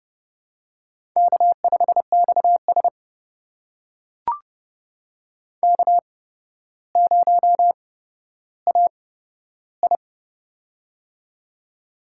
Morse code, heard as a keyed tone, about 30 wpm, spaced as for 13 wpm, 700 Hz. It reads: K5XS K0AI
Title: USA Call Signs - 4 Characters - 8x Character Spacing - 30wpm